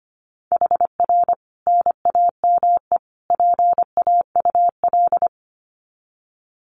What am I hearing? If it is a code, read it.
Morse code, heard as a keyed tone, about 25 wpm, 700 Hz. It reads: HR NAME PAUL
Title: Common QSO Phrases 25wpm